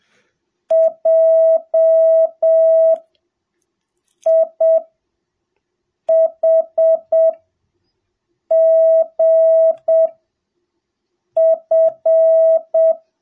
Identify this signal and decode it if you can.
Morse code, heard as a keyed tone, about 7 wpm, spaced as for 4 wpm, 660 Hz. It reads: JIHGF